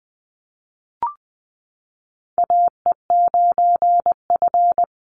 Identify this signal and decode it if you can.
Morse code, heard as a keyed tone, about 20 wpm, 700 Hz. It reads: AE9F
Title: NA SOTA Chaser Callsigns 20wpm